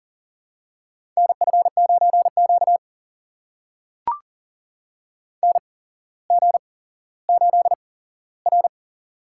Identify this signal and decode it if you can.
Morse code, heard as a keyed tone, about 40 wpm, spaced as for 17 wpm, 700 Hz. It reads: NF9Q NG8R
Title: USA Call Signs - 4 Characters - 8x Character Spacing - 40wpm